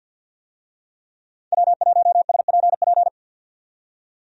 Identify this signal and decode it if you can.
Morse code, heard as a keyed tone, about 50 wpm, 700 Hz. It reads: W1SPP
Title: USA Call Signs - 5 Characters - 50wpm